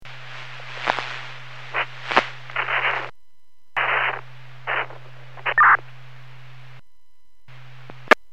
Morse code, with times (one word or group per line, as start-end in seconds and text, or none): none